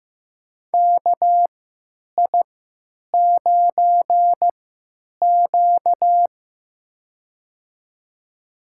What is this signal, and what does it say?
Morse code, heard as a keyed tone, about 15 wpm, spaced as for 9 wpm, 700 Hz. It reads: KI9Q
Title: USA Call Signs - 4 Characters - 3x Character Spacing - 15wpm